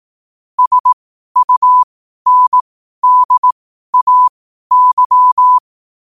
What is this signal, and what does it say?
Morse code, heard as a keyed tone, about 18 wpm, spaced as for 12 wpm, 1000 Hz. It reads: SUNDAY
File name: qotdfiles / 12wpm